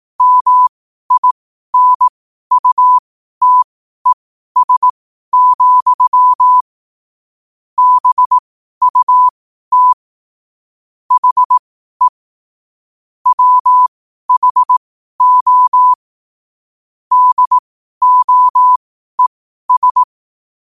Morse code, heard as a keyed tone, about 18 wpm, spaced as for 12 wpm, 1000 Hz. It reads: MINUTES, BUT HE WHO DOES